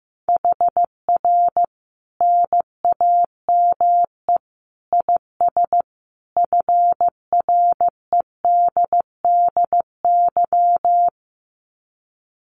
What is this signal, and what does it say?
Morse code, heard as a keyed tone, about 15 wpm, 700 Hz. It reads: HR NAME IS FREDDY
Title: Common QSO Phrases 15wpm